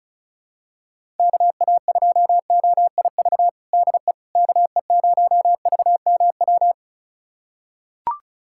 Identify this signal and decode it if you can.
Morse code, heard as a keyed tone, about 36 wpm, 700 Hz. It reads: KA2OIV DE KE0VMW
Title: USA Callsign DE Callsign - 6 Characters - 35wpm